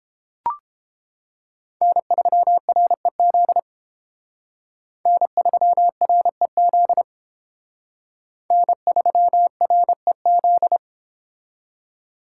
Morse code, heard as a keyed tone, about 30 wpm, 700 Hz. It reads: N3REZ N3REZ N3REZ